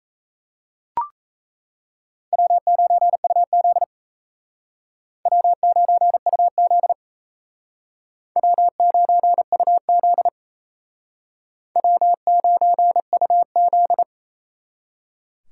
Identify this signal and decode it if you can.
Morse code, heard as a keyed tone, about 35 wpm, 700 Hz. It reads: W9UZ W9UZ W9UZ W9UZ
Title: Speed Racing - USA Call Signs 28wpm